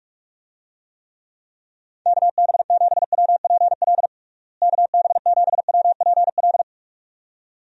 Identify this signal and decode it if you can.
Morse code, heard as a keyed tone, about 41 wpm, 700 Hz. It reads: KB7WPL KB7WPL
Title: USA Call Signs - 6 Characters - Repeated 2x - 45wpm